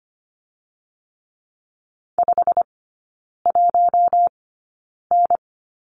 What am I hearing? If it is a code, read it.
Morse code, heard as a keyed tone, about 25 wpm, spaced as for 9 wpm, 690 Hz. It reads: 51N
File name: Lesson 15e - Number 5 - Three Characters 25wpm